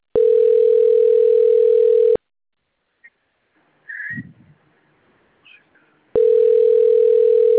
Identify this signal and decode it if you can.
Ringback tone (North America), 440 + 480 Hz.